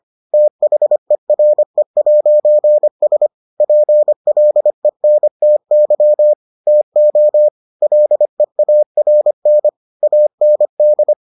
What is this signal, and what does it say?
Morse code, heard as a keyed tone, about 25 wpm, 600 Hz. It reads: THERE'S PLENTY TO LEARN AND